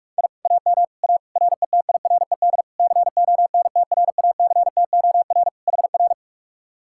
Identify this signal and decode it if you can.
Morse code, heard as a keyed tone, about 49 wpm, 700 Hz. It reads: I AM A RETIRED CONTRACTOR HR